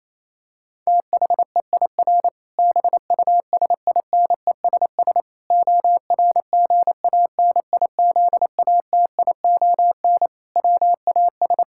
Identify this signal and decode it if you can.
Morse code, heard as a keyed tone, about 28 wpm, 700 Hz. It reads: THEIR BUSINESS ORGANIZATION WAS
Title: Sentences from Top 1000 Words - Review Part 5 - 28wpm